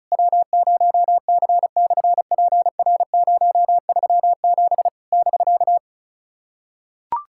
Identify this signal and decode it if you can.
Morse code, heard as a keyed tone, about 33 wpm, 700 Hz. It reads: W0C/PR037 <BK>
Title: SOTA QSO Practice 35wpm